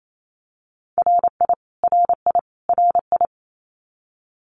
Morse code, heard as a keyed tone, about 28 wpm, 700 Hz. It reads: RI RI RI
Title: US States Repeated 3x 28wpm